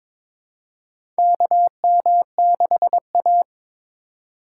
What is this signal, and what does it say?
Morse code, heard as a keyed tone, about 22 wpm, 700 Hz. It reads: KM6A
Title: USA Call Signs - 2x Character Spacing - 22wpm